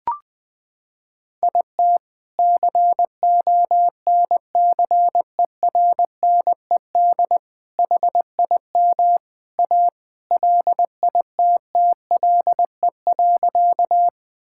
Morse code, heard as a keyed tone, about 20 wpm, 700 Hz. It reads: IT CONCERNED HIM A LITTLE.
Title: Sentences from Top 1000 Words - Review Part 6 - 20wpm